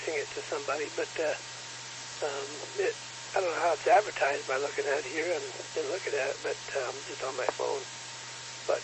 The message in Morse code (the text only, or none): none